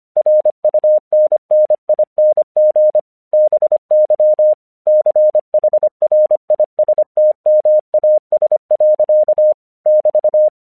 Morse code, heard as a keyed tone, about 25 wpm, 610 Hz. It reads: RUNNING BY CHRISTMAS. <BT>
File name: 1 English 25wpm / AQC24